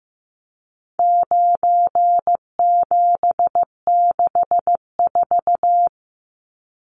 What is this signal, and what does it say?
Morse code, heard as a keyed tone, about 15 wpm, 700 Hz. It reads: 9764
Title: Numbers - 4-Digit 15wpm